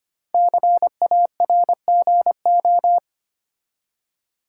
Morse code, heard as a keyed tone, about 25 wpm, 700 Hz. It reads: CARGO
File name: 5 Letter Words - First 3 Letters Given - 25wpm